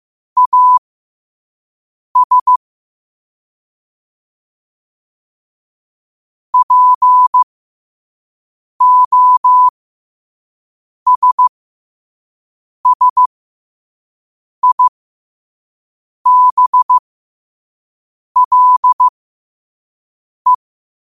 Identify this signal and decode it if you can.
Morse code, heard as a keyed tone, about 15 wpm, spaced as for 5 wpm, 1000 Hz. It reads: AS POSSIBLE
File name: qotdfiles / 5wpm